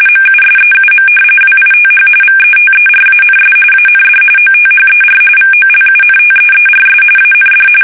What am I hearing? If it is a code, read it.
RTTY, about 45.45 baud. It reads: MUSEUM WITH VETERANS AND OTHER VOLUNTEERS AND
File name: fsk850 / battleshipnj